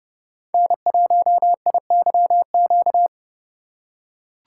Morse code, heard as a keyed tone, about 30 wpm, 700 Hz. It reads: N1IYQ
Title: USA Call Signs - 4x Character Spacing - 30wpm